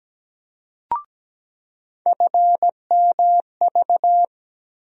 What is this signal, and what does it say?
Morse code, heard as a keyed tone, about 17 wpm, 700 Hz. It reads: FMV
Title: USA Call Sign Postfixes - 3 Letters 17wpm